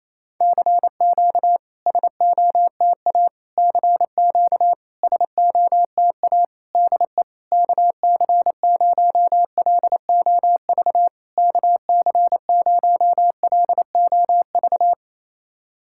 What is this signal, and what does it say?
Morse code, heard as a keyed tone, about 28 wpm, 700 Hz. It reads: CQ SOTA CQ SOTA DE KC0LOV KC0LOV
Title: SOTA QSO Practice 28wpm